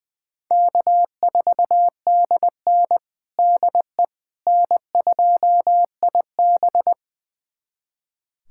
Morse code, heard as a keyed tone, about 21 wpm, 700 Hz. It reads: K4DN DE N2IB
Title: USA Callsign DE Callsign - 4 Characters - 20wpm